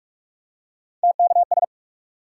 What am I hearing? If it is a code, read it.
Morse code, heard as a keyed tone, about 44 wpm, 700 Hz. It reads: TKS
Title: Top 180 QSO Elements 45wpm